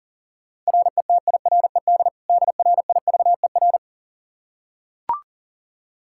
Morse code, heard as a keyed tone, about 40 wpm, 700 Hz. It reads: RETIRED DRIVER
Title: Practice QSO - Exchange 3 - More Info - 40wpm